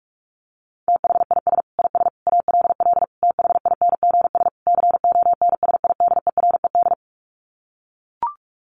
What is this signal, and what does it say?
Morse code, heard as a keyed tone, about 44 wpm, 700 Hz. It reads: THIS IS ALL THINGS CONSIDERED
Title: Sets of 5 Words 45wpm